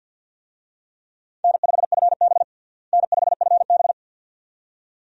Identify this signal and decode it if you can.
Morse code, heard as a keyed tone, about 46 wpm, 700 Hz. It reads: N5FB N5FB
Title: USA Call Signs - 4 Characters - Repeated 2x - 50wpm